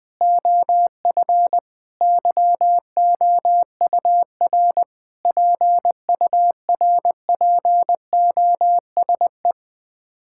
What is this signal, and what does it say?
Morse code, heard as a keyed tone, about 21 wpm, 700 Hz. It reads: OF YOUR PURPOSE
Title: Sets of 9 Words - 20wpm